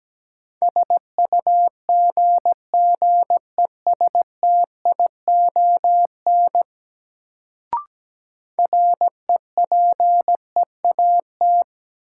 Morse code, heard as a keyed tone, about 17 wpm, 700 Hz. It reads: SUGGESTION REPEAT
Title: Sentences from Top 1600 Words - New Word Drill - 17wpm